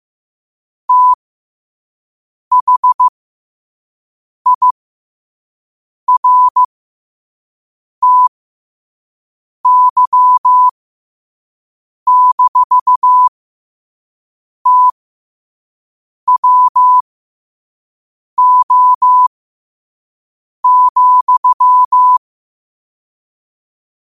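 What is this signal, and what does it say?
Morse code, heard as a keyed tone, about 15 wpm, spaced as for 5 wpm, 1000 Hz. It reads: THIRTY-TWO,
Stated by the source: Morse quote-of-the-day practice at 5 wpm.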